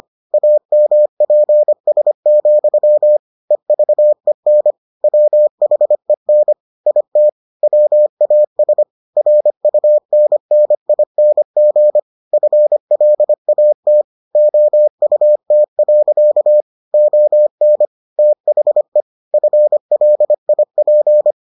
Morse code, heard as a keyed tone, about 25 wpm, 600 Hz. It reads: AMPS, EVEN WHEN IT WAS RUNNING FLAT OUT. ON THE FLIP